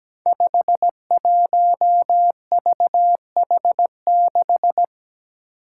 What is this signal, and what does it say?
Morse code, heard as a keyed tone, about 17 wpm, 700 Hz. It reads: 51VH6
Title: Five Challenging Letter-Number Combinations 17wpm